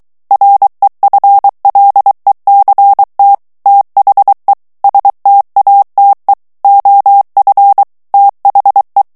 Morse code, heard as a keyed tone, about 23 wpm, 800 Hz. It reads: REFLECT THE STATE OF THE
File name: ANGLAIS texte clair 3 vitesse 22 mots minute